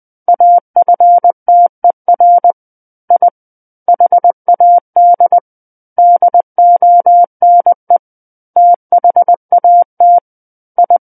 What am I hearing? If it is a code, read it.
Morse code, heard as a keyed tone, about 20 wpm, 700 Hz. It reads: AFTER I HAD DONE THAT I
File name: 20 WPM